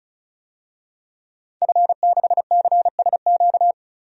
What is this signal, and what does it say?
Morse code, heard as a keyed tone, about 36 wpm, 700 Hz. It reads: F6CSQ
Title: DXCC-100 Call Signs - 5 Characters - 35wpm